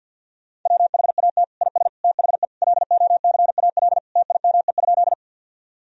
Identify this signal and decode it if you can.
Morse code, heard as a keyed tone, about 50 wpm, 700 Hz. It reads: WHAT IS THE LOCAL TIME?
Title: Sentences from Top 1000 Words - Review Part 3 - 50wpm